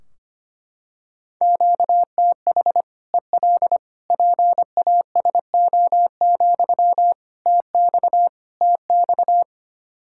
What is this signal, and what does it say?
Morse code, heard as a keyed tone, about 25 wpm, 700 Hz. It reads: QTH EL PASO, TX TX
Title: Practice QSO - Exchange 1 - Beginning - 25wpm